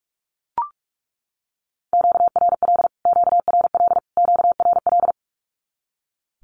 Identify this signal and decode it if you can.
Morse code, heard as a keyed tone, about 44 wpm, 700 Hz. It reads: QRL QRL QRL
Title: Q-Codes Repeated 3x 45wpm